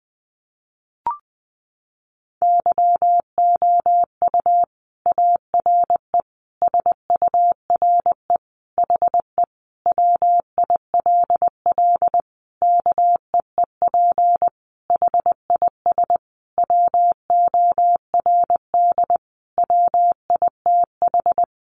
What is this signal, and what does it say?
Morse code, heard as a keyed tone, about 20 wpm, 700 Hz. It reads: YOU ARE SURE HE WILL KEEP HIS WORD WITH
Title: Sentences from Top 300 Words 20wpm